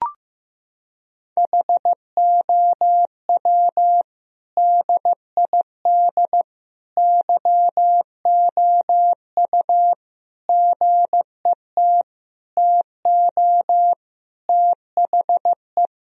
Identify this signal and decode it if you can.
Morse code, heard as a keyed tone, about 15 wpm, 700 Hz. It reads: HOW DID YOU GET TO THE
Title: Sentences from Top 1000 Words - Review Part 3 - 15wpm